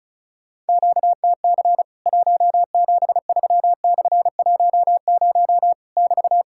Morse code, heard as a keyed tone, about 36 wpm, 700 Hz. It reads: QTC 173/10 <BT>
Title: Worked All Europe DX Contest - QTC Exchange - 35wpm